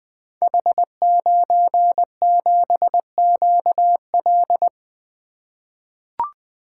Morse code, heard as a keyed tone, about 20 wpm, 700 Hz. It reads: H97QL
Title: Code Groups - 5 Letter:Numbers - 20wpm